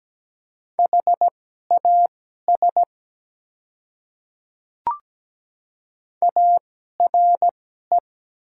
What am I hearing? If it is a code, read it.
Morse code, heard as a keyed tone, about 17 wpm, spaced as for 12 wpm, 700 Hz. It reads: HAS ARE
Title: 2x Character Spacing - Top 100 Words - 17wpm